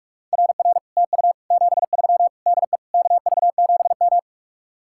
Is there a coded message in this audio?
Yes: Morse code from 0.3 to 4.2 s, heard as a keyed tone, about 49 wpm, 700 Hz. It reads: RR TU 73 DE KV7M